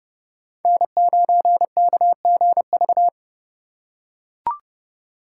Morse code, heard as a keyed tone, about 27 wpm, 700 Hz. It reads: N9KGV